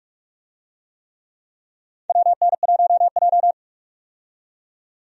Morse code, heard as a keyed tone, about 43 wpm, 700 Hz. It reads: WN1J
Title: USA Call Signs - 4 Characters - 45wpm